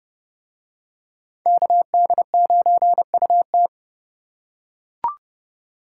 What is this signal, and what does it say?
Morse code, heard as a keyed tone, about 29 wpm, 700 Hz. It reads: KD9UT